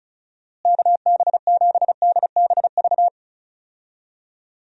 Morse code, heard as a keyed tone, about 34 wpm, 700 Hz. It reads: KB7DBV
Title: USA Call Signs - 6 Characters - Repeated 2x - 35wpm